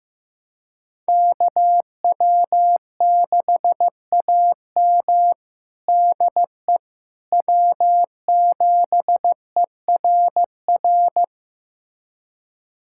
Morse code, heard as a keyed tone, about 15 wpm, 700 Hz. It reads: KW6AM DE W7ERR
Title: USA Callsign DE Callsign - 5 Characters - 15wpm